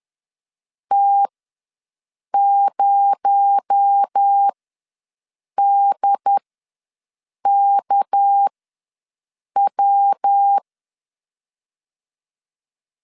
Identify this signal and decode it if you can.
Morse code, heard as a keyed tone, about 10 wpm, spaced as for 6 wpm, 800 Hz. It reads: T0DKW
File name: LEZIONI / 7 GF9Z / 7 GF ws 10wpm 3918